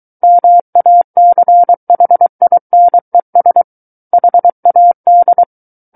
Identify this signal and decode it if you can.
Morse code, heard as a keyed tone, about 23 wpm, 700 Hz. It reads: MACHINES HAD
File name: Book One / 14-Chapter 14 - 23 WPM